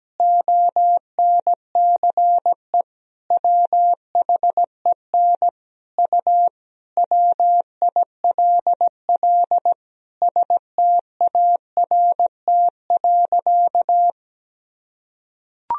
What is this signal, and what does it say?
Morse code, heard as a keyed tone, about 17 wpm, 700 Hz. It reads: ONCE WHEN U WILL START.
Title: Sentences from Top 300 Words and Abbreviations 17wpm